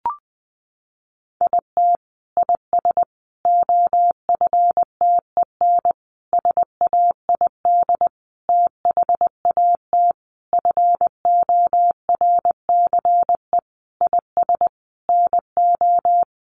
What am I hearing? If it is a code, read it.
Morse code, heard as a keyed tone, about 19 wpm, 700 Hz. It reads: IT IS OFTEN SAID THAT FORCE IS NO
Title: Sentences from Top 1400 Words 20wpm